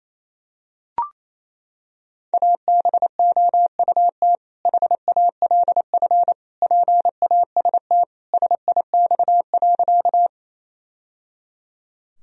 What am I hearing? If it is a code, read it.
Morse code, heard as a keyed tone, about 27 wpm, 700 Hz. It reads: ABOUT HALF PAST SIX.
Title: Sentences from Top 1000 Words - Review Part 3 - 28wpm